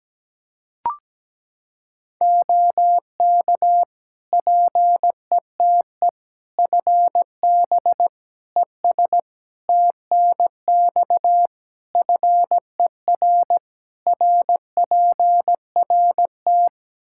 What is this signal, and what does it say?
Morse code, heard as a keyed tone, about 17 wpm, 700 Hz. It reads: OK PETE FB ES TNX FER RPRT
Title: Practice QSO - Exchange 2 - Info - 17wpm